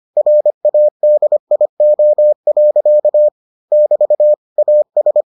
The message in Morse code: RADIO. <BT> AS

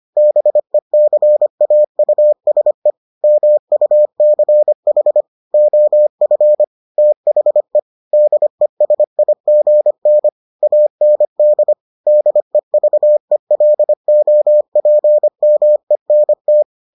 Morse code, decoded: BECAUSE MUCH OF THE DESIGN AND DEVELOPMENT

